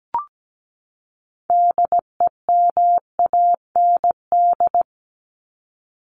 Morse code, heard as a keyed tone, about 17 wpm, 700 Hz. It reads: DEMAND
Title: Top 501-750 Words 17wpm